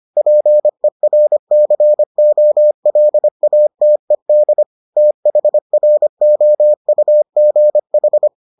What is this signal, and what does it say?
Morse code, heard as a keyed tone, about 25 wpm, 600 Hz. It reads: PERCOLATED THROUGH